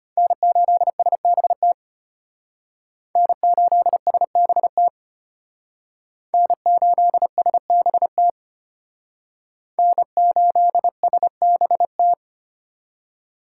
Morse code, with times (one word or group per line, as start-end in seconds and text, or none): 0.2-1.7 N8SBT
3.2-4.9 N8SBT
6.3-8.3 N8SBT
9.8-12.1 N8SBT